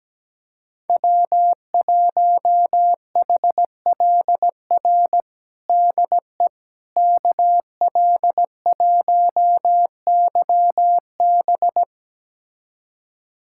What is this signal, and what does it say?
Morse code, heard as a keyed tone, about 17 wpm, 700 Hz. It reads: W1HLR DE KL1YB
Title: USA Callsign DE Callsign - 5 Characters - 17wpm